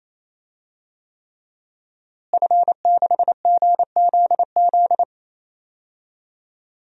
Morse code, heard as a keyed tone, about 28 wpm, 700 Hz. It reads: F6GZZ
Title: DXCC-100 Call Signs - 5 Characters - Repeated 3x - 28wpm